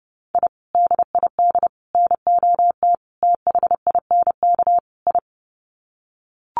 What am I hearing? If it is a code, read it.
Morse code, heard as a keyed tone, about 30 wpm, 700 Hz. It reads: I DID NOT THINK I
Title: Sets of 5 Words Encore 30wpm